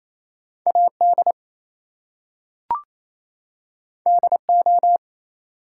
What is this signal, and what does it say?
Morse code, heard as a keyed tone, about 27 wpm, 700 Hz. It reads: AD DO
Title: Two Letter Words - 28wpm